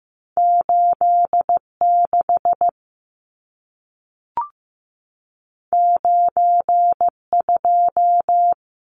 Morse code, heard as a keyed tone, about 15 wpm, 700 Hz. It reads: 86 92
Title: Numbers - 2-Digit 15wpm